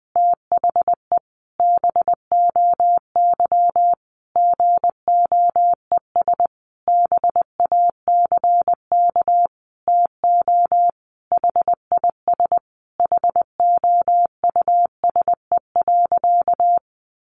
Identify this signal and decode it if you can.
Morse code, heard as a keyed tone, about 20 wpm, 700 Hz. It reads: THE BOY GOES BACK TO HIS HOUSE.
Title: Easy Phrases 20wpm